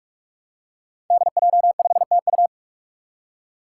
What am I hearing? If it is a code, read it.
Morse code, heard as a keyed tone, about 44 wpm, 700 Hz. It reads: DJ5TU
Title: Five Challenging Letter-Number Combinations 45wpm